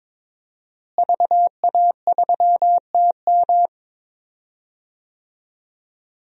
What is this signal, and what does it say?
Morse code, heard as a keyed tone, about 22 wpm, 700 Hz. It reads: VA3TM
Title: DXCC-100 Call Signs - 5 Characters - Repeated 2x - 22wpm